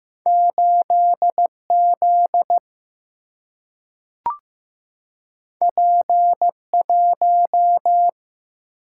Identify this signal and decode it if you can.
Morse code, heard as a keyed tone, about 15 wpm, 700 Hz. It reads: 8Z P1
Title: Single-Letter Single-Number 15wpm